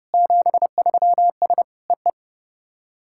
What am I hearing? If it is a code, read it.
Morse code, heard as a keyed tone, about 30 wpm, 690 Hz. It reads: 73S EE
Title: SOTA QSO Practice 30wpm